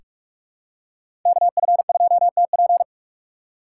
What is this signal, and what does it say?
Morse code, heard as a keyed tone, about 46 wpm, 700 Hz. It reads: KF2TP
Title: K1USN SST QSO Practice - 45wpm